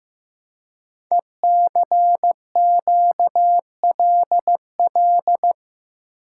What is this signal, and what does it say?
Morse code, heard as a keyed tone, about 15 wpm, 700 Hz. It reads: ECQLL
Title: Code Groups - 5 Letters - 15wpm